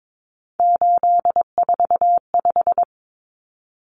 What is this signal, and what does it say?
Morse code, heard as a keyed tone, about 22 wpm, 700 Hz. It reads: 845
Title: Common Cut Numbers - 3-Digit 22wpm